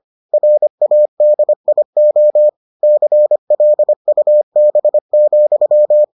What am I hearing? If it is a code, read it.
Morse code, heard as a keyed tone, about 25 wpm, 600 Hz. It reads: RADIO CLUB,